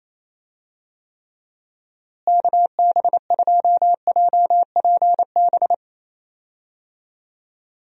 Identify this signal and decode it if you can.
Morse code, heard as a keyed tone, about 28 wpm, 700 Hz. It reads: KB2JPB
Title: USA Call Signs - 6 Characters - Repeated 3x - 28wpm